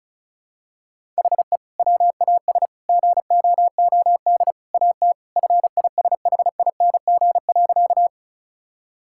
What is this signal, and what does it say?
Morse code, heard as a keyed tone, about 34 wpm, 700 Hz. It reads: HE WAS GOOD AT FISHING.
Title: Sentences from Top 1000 Words - Review Part 5 - 35wpm